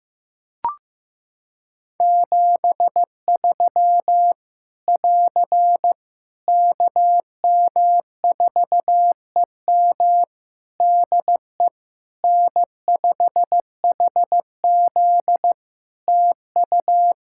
Morse code, heard as a keyed tone, about 15 wpm, 700 Hz. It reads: 73 <AR> KM4EM DE N5HZ TU
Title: Practice QSO - Exchange 4 - Ending - 15wpm